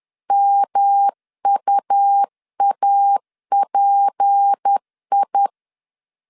Morse code, heard as a keyed tone, about 10 wpm, 800 Hz. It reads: MUAPI